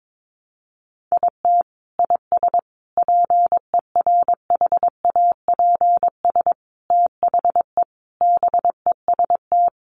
Morse code, heard as a keyed tone, about 22 wpm, 700 Hz. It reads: IT IS PERHAPS THE BEST
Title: Sentences from Top 1300 Words 22wpm